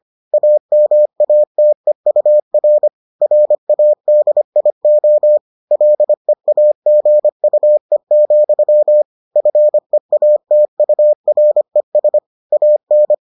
Morse code, heard as a keyed tone, about 25 wpm, 600 Hz. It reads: AMATEUR RADIO LEAGUE, FEATURES AN